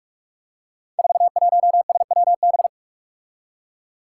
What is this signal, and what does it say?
Morse code, heard as a keyed tone, about 43 wpm, 700 Hz. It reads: 41SWB
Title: Five Challenging Letter-Number Combinations 45wpm